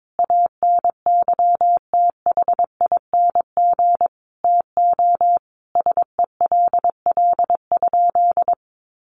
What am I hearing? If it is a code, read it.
Morse code, heard as a keyed tone, about 22 wpm, 700 Hz. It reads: ANYTHING TO SELL?
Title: Sentences from Top 1100 Words 22wpm